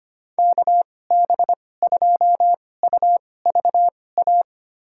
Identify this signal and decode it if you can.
Morse code, heard as a keyed tone, about 25 wpm, spaced as for 18 wpm, 700 Hz. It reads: KB2UVA
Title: USA Call Signs - 6 Characters - 2x Character Spacing - 25wpm